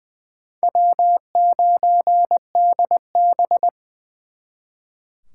Morse code, heard as a keyed tone, about 24 wpm, 700 Hz. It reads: W9DB
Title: Speed Racing - USA Call Signs 20wpm